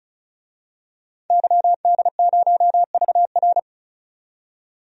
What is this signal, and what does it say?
Morse code, heard as a keyed tone, about 34 wpm, 700 Hz. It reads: YD0VR
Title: DXCC-100 Call Signs - 5 Characters - 35wpm